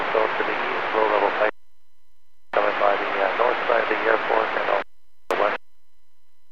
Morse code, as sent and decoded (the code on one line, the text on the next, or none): none